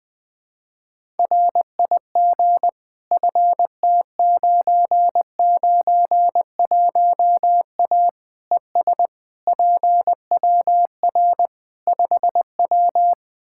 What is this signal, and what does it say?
Morse code, heard as a keyed tone, about 20 wpm, 700 Hz. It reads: RIG FT991A ES PWR 5W
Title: Practice QSO - Exchange 2 - Info - 20wpm